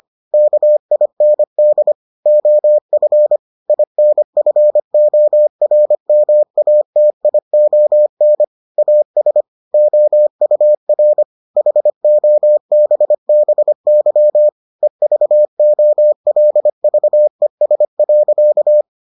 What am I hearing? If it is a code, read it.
Morse code, heard as a keyed tone, about 25 wpm, 600 Hz. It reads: KIND OF INFORMATION AS OUR HOBBY EVOLVES.